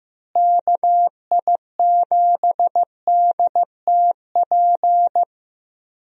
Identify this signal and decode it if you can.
Morse code, heard as a keyed tone, about 15 wpm, 700 Hz. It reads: KI7DTP
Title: USA Call Signs - 6 Characters - Repeated 3x - 15wpm